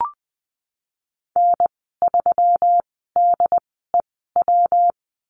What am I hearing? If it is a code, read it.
Morse code, heard as a keyed tone, about 20 wpm, spaced as for 15 wpm, 700 Hz. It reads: N3DEW
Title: 122 Lesson 20g - Number 3 - Call Signs 20wpm